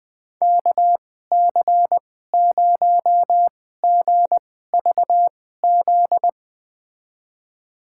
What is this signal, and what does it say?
Morse code, heard as a keyed tone, about 20 wpm, spaced as for 14 wpm, 700 Hz. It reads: KC0GVZ
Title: USA Call Signs - 6 Characters - 2x Character Spacing - 20wpm